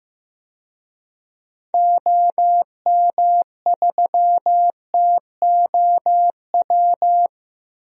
Morse code, heard as a keyed tone, about 15 wpm, 700 Hz. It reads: OM3TOW
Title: DXCC-100 Call Signs - 6 Characters - Repeated 2x - 15wpm